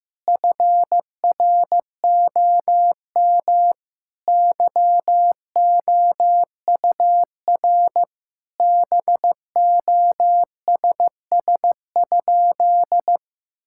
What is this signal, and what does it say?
Morse code, heard as a keyed tone, about 15 wpm, 700 Hz. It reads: FROM YOUR BOSS?
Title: Sentences from Top 1000 Words - Review Part 6 - 15wpm